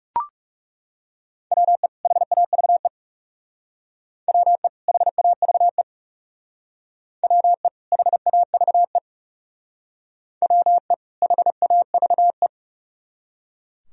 Morse code, heard as a keyed tone, about 36 wpm, 700 Hz. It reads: WE HAVE WE HAVE WE HAVE WE HAVE
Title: Speed-Racing - Sets of 2 Words 30wpm